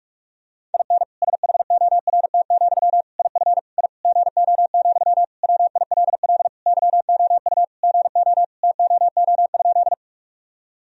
Morse code, heard as a keyed tone, about 46 wpm, 700 Hz. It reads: IN SHORT, IF I GO, WILL YOU GO TOO?
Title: Sentences from Top 1000 Words - Review Part 2 - 45wpm